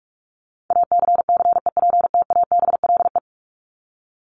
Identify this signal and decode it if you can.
Morse code, heard as a keyed tone, about 44 wpm, 700 Hz. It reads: ACCEPTABLE
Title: Top 1501-1750 Words 45wpm